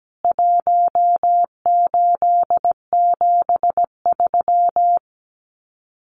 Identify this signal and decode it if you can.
Morse code, heard as a keyed tone, about 17 wpm, 700 Hz. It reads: 1873
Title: CWT Exchange 17wpm